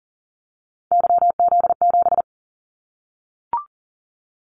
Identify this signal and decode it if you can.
Morse code, heard as a keyed tone, about 40 wpm, 700 Hz. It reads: YZ7
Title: Callsign Prefixes for Top 25 DXCC - 40wpm